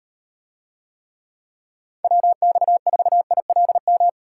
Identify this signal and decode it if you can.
Morse code, heard as a keyed tone, about 36 wpm, 700 Hz. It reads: WX4ILM